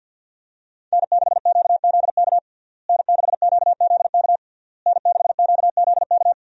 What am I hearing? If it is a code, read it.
Morse code, heard as a keyed tone, about 52 wpm, 700 Hz. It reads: N6QZK N6QZK N6QZK
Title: USA Call Signs - 5 Characters - Repeated 3x - 50wpm